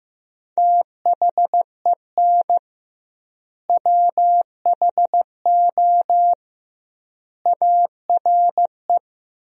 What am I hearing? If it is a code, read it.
Morse code, heard as a keyed tone, about 15 wpm, 700 Hz. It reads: THEN WHO ARE
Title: 2x Word-Spacing - Sentences from Top 100 Words - 15wpm